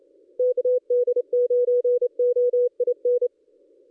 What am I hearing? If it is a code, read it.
Morse code, heard as a keyed tone, about 27 wpm, 490 Hz. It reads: KD9OIN
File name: contests / ss-cw-k5tr-op-n5rz -1400